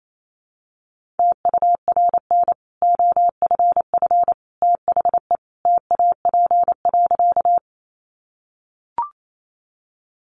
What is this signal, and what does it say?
Morse code, heard as a keyed tone, about 28 wpm, 700 Hz. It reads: TURN OFF THE TAP.
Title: Sentences from Top 1400 Words 28wpm